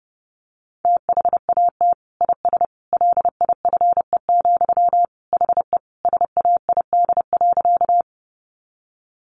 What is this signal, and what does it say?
Morse code, heard as a keyed tone, about 30 wpm, 700 Hz. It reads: THAT IS LIFE, HE SAID.
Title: Sentences from Top 900 Words 30wpm